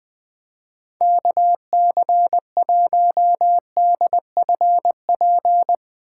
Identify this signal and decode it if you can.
Morse code, heard as a keyed tone, about 19 wpm, 700 Hz. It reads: KC1DFP
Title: Speed Racing - USA 2x3 Call Signs 17wpm